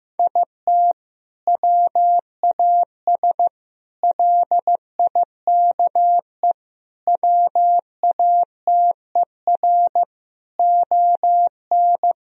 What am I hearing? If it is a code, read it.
Morse code, heard as a keyed tone, about 16 wpm, 700 Hz. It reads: IT WAS LIKE WATER ON
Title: Sentences from Top 1000 Words - Review Part 4 - 15wpm